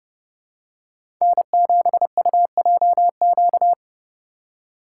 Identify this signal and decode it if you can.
Morse code, heard as a keyed tone, about 29 wpm, 700 Hz. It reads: N7UJQ
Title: USA Call Signs - 5 Characters - 2x Character Spacing - 30wpm